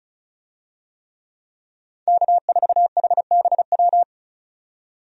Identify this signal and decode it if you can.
Morse code, heard as a keyed tone, about 36 wpm, 700 Hz. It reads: K4HBW
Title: USA Call Signs - 5 Characters - 35wpm